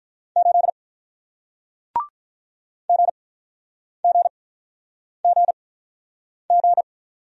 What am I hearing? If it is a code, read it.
Morse code, heard as a keyed tone, about 48 wpm, spaced as for 9 wpm, 700 Hz. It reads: 8 GGGG